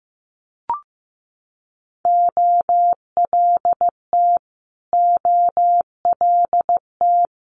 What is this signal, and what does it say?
Morse code, heard as a keyed tone, about 15 wpm, 700 Hz. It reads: OLT OLT
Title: Multi-letter Phonemes Repeated 3x 15wpm